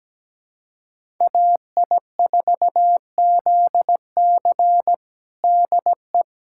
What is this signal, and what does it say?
Morse code, heard as a keyed tone, about 17 wpm, 700 Hz. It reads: AI4ZC DE
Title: USA Callsign DE Callsign - 5 Characters - 17wpm